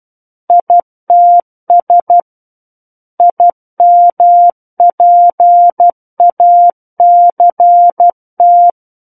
Morse code, heard as a keyed tone, about 12 wpm, 700 Hz. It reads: ITS IMPACT
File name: Book One / 03-Chapter 3 - 12 WPM